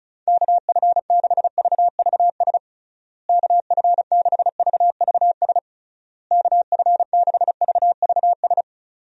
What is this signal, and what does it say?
Morse code, heard as a keyed tone, about 36 wpm, 700 Hz. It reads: KF6VVS KF6VVS KF6VVS
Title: USA Call Signs - 6 Characters - Repeated 3x - 35wpm